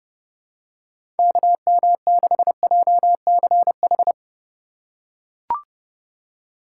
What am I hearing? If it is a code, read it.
Morse code, heard as a keyed tone, about 32 wpm, 700 Hz. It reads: KM6JCH
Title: USA Call Signs - 2x Character Spacing - 30wpm